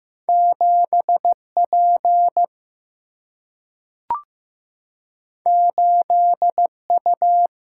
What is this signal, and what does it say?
Morse code, heard as a keyed tone, about 15 wpm, 700 Hz. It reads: 7P 8U
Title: Single-Letter Single-Number 15wpm